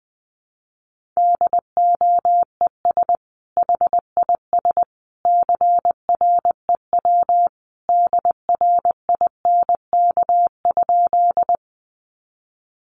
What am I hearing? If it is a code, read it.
Morse code, heard as a keyed tone, about 20 wpm, 700 Hz. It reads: DOES HIS CREW DRINK?
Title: Sentences from Top 1500 Words 20wpm